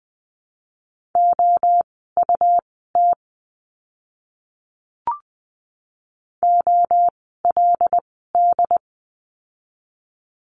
Morse code, heard as a keyed tone, about 20 wpm, spaced as for 15 wpm, 700 Hz. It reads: OUT OLD
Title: 244 Lesson 39f - Letter z - Words 20wpm